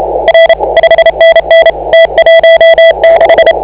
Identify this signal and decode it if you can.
Morse code, heard as a keyed tone, about 30 wpm, 660 Hz, with noise. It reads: R 5NN T16